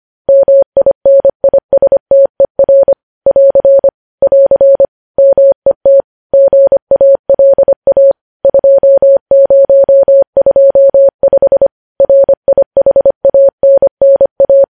Morse code, heard as a keyed tone, about 25 wpm, 550 Hz. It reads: MINISTER <AR> <AR> MET GALA 2025 RIHANNA
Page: news-headlines-in-morse-code-at-25-wpm